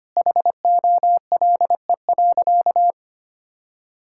Morse code, heard as a keyed tone, about 25 wpm, 700 Hz. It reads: HOLE.
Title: Sentences from Top 1600 Words 25wpm